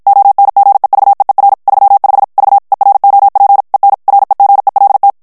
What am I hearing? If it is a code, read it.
Morse code, heard as a keyed tone, about 56 wpm, 800 Hz. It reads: ONGEVEER 35 V EROP EN REGELT
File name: HOLLANDAIS texte clair 2 vitesse 50 mots minute